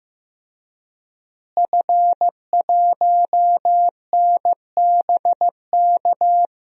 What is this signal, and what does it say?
Morse code, heard as a keyed tone, about 15 wpm, 700 Hz. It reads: F1NBK
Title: DXCC-100 Call Signs - 5 Characters - Repeated 2x - 15wpm